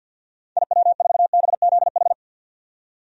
Morse code, heard as a keyed tone, about 47 wpm, 700 Hz. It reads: IW4BZH
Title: DXCC-100 Call Signs - 6 Characters - 50wpm